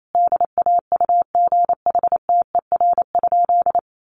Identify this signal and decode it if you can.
Morse code, heard as a keyed tone, about 28 wpm, 700 Hz. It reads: DAUGHTER?
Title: Sentences from Top 1500 Words 28wpm